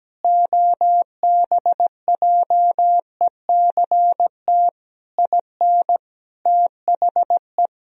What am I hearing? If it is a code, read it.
Morse code, heard as a keyed tone, about 17 wpm, 700 Hz. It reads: OBJECT IN THE
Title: Sentences from Top 1000 Words - Review Part 3 - 17wpm